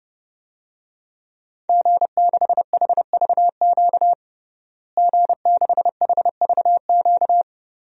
Morse code, heard as a keyed tone, about 30 wpm, 700 Hz. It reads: G6HVQ G6HVQ
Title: DXCC-100 Call Signs - 5 Characters - Repeated 3x - 30wpm